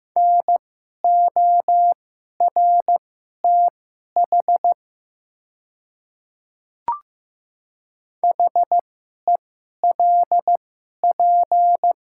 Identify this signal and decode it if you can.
Morse code, heard as a keyed tone, about 15 wpm, spaced as for 11 wpm, 700 Hz. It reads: NORTH HELP
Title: 2x Character Spacing - Top 300 Words - 15wpm